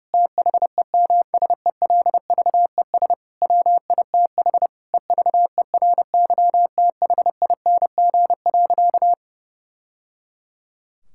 Morse code, heard as a keyed tone, about 30 wpm, 700 Hz. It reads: THEMSELVES WITH EVERYTHING.
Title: Sentences from Top 1000 Words - Review Part 5 - 30wpm